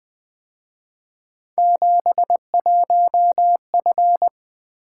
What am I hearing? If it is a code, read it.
Morse code, heard as a keyed tone, about 23 wpm, 700 Hz. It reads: 71F
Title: Speed-Racing - Three Letter-Number Combinations - 20wpm